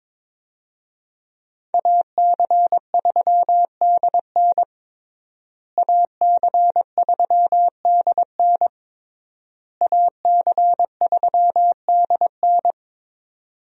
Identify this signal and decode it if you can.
Morse code, heard as a keyed tone, about 22 wpm, 700 Hz. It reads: AC3DN AC3DN AC3DN
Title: USA Call Signs - 5 Characters - Repeated 3x - 22wpm